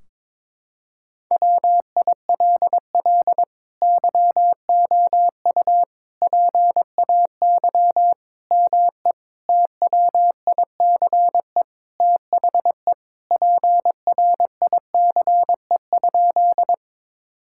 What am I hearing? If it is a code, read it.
Morse code, heard as a keyed tone, about 22 wpm, 700 Hz. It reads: WILL YOU PAY ME TWICE THE PRICE?
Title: Sentences from Top 1600 Words 22wpm